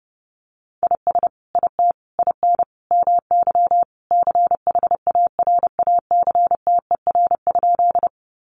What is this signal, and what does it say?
Morse code, heard as a keyed tone, about 30 wpm, 700 Hz. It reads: IS IT IN MY CHARACTER?
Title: Sentences from Top 1200 Words 30wpm